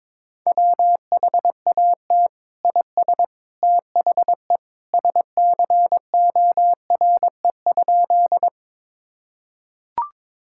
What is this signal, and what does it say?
Morse code, heard as a keyed tone, about 22 wpm, 700 Hz. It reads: WHAT IS THE SCORE?
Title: Sentences from Top 1600 Words 22wpm